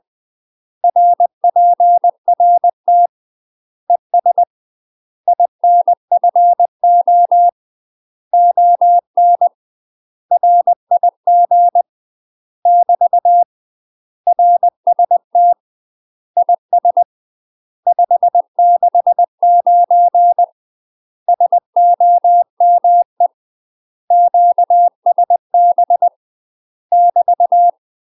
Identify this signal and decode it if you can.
Morse code, heard as a keyed tone, about 20 wpm, 700 Hz. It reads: RPRT ES INFO ON RIG <BT> RST IS 569 SOME QSB <BT>
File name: UK MASTER QSOs TRAINING / 20WPM / UK MASTER QSOs 0042